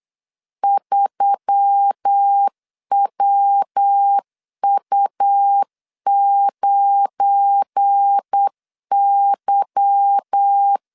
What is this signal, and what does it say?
Morse code, heard as a keyed tone, about 8 wpm, 800 Hz. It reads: 3WU9Y
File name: LEZIONI / 8 YJ45 / 8 Y 8wpm 3245